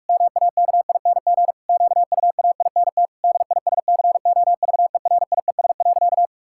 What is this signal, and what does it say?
Morse code, heard as a keyed tone, about 42 wpm, 700 Hz. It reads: MAKING QUAINT DISCOVERIES.
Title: Pangram Challenge v2 - 45wpm